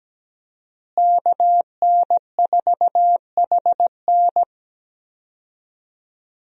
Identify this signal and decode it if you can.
Morse code, heard as a keyed tone, about 17 wpm, 700 Hz. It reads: KN4HN
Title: USA Call Signs - 5 Characters - Repeated 3x - 17wpm